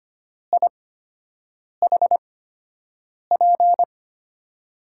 Morse code, heard as a keyed tone, about 25 wpm, spaced as for 7 wpm, 700 Hz. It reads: IHP
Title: USA Call Signs - 5 Characters - 8x Character Spacing - 25wpm